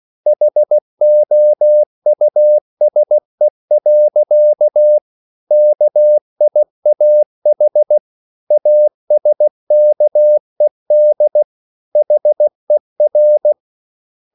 Morse code, heard as a keyed tone, about 16 wpm, 600 Hz. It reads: HOUSE. KIAH ASKED HER